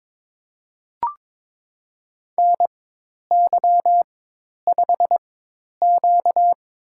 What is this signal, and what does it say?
Morse code, heard as a keyed tone, about 22 wpm, spaced as for 10 wpm, 700 Hz. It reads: NY5Q
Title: USA Call Signs - 4 Characters - 4x Character Spacing - 22wpm